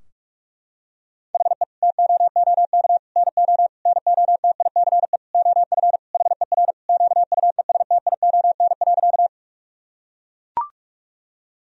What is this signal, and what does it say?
Morse code, heard as a keyed tone, about 41 wpm, 700 Hz. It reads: HE TOOK NO NOTICE OF HER QUESTION.
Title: Sentences from Top 1000 Words - Review Part 2 - 45wpm